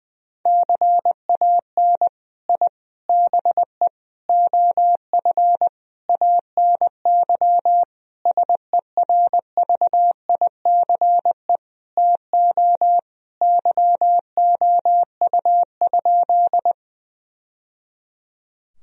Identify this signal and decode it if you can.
Morse code, heard as a keyed tone, about 20 wpm, 700 Hz. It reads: CAN I BE OF ANY SERVICE TO YOU?
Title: Sentences from Top 1000 Words - Review Part 4 - 20wpm